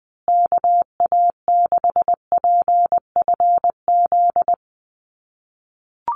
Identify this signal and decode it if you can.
Morse code, heard as a keyed tone, about 20 wpm, 700 Hz. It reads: KA6PFZ
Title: USA Call Signs Repeated 2x 20wpm